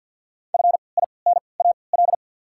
Morse code, heard as a keyed tone, about 53 wpm, spaced as for 29 wpm, 690 Hz. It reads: FINAL
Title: 3x Character Spacing - Top 500 Words - 50wpm